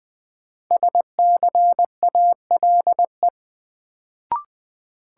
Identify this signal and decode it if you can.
Morse code, heard as a keyed tone, about 21 wpm, 700 Hz. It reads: SCALE
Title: Sentences from Top 1000 Words - Review Part 4 - New Word Drill - 20wpm